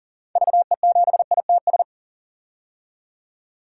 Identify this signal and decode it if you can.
Morse code, heard as a keyed tone, about 40 wpm, 700 Hz. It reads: VE7ITS
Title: DXCC-100 Call Signs - 6 Characters - 40wpm